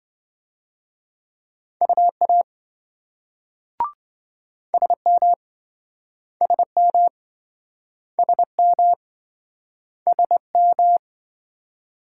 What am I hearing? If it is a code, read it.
Morse code, heard as a keyed tone, about 28 wpm, 700 Hz. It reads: UA SM SM SM SM